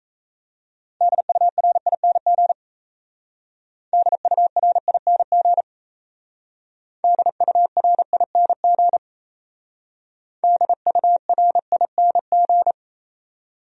Morse code, heard as a keyed tone, about 34 wpm, 700 Hz. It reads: DURING DURING DURING DURING
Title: Speed-Racing Top 500 Words 28wpm